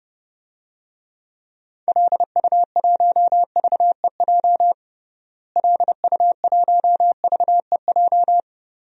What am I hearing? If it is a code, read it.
Morse code, heard as a keyed tone, about 30 wpm, 700 Hz. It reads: LU1VEJ LU1VEJ
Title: DXCC-100 Call Signs - 6 Characters - Repeated 3x - 30wpm